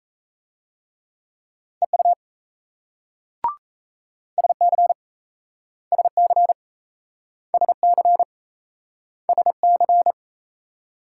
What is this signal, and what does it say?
Morse code, heard as a keyed tone, about 37 wpm, 700 Hz. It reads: EU SC SC SC SC